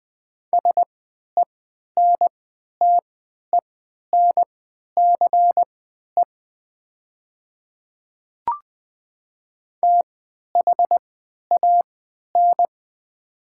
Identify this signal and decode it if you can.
Morse code, heard as a keyed tone, about 20 wpm, spaced as for 11 wpm, 700 Hz. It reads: SENTENCE THAN
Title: 3x Character Spacing - Top 300 Words - 20wpm